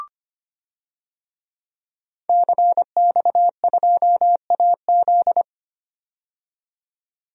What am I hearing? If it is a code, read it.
Morse code, heard as a keyed tone, about 25 wpm, 700 Hz. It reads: CX2AZ
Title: DXCC-100 Call Signs - 5 Characters - 25wpm